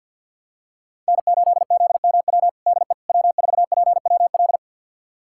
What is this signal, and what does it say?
Morse code, heard as a keyed tone, about 52 wpm, 700 Hz. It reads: N8ZMW DE W4PWL
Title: USA Callsign DE Callsign - 5 Characters - 50wpm